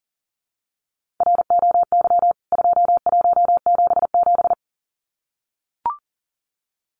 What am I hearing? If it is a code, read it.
Morse code, heard as a keyed tone, about 40 wpm, 700 Hz. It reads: ROY 2177
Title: CWT Exchange 40wpm